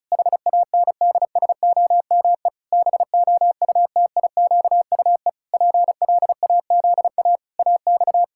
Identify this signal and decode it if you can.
Morse code, heard as a keyed tone, about 34 wpm, 700 Hz. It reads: HANDSOME BOUTIQUE PLAZA AX
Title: Pangram Challenge v2 - 35wpm